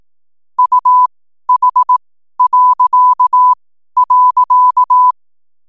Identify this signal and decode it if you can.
Morse code, heard as a keyed tone, about 18 wpm, spaced as for 12 wpm, 1000 Hz. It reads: UH..
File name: n7net / 12wpm